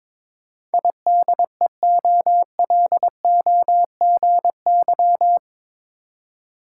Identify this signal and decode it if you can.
Morse code, heard as a keyed tone, about 22 wpm, 700 Hz. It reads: IDEOLOGY
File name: practice / 22wpm / Eight Letter Words - 22wpm